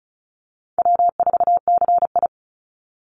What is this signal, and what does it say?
Morse code, heard as a keyed tone, about 34 wpm, 700 Hz. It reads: W4CI
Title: Easy Call Signs Repeated 3x - 35wpm